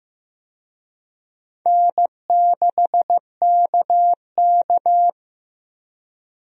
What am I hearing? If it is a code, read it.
Morse code, heard as a keyed tone, about 15 wpm, 700 Hz. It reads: N6KK
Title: USA Call Signs - 4 Characters - Repeated 2x - 15wpm